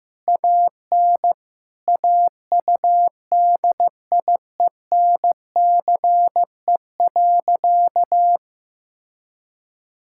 Morse code, heard as a keyed tone, about 15 wpm, 700 Hz. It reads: AN AUDIENCE.
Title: Sentences from Top 1000 Words - Review Part 5 - 15wpm